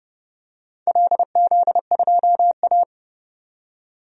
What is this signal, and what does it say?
Morse code, heard as a keyed tone, about 30 wpm, 700 Hz. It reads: LZ2A
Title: DXCC-100 Call Signs - 4 Characters - 30wpm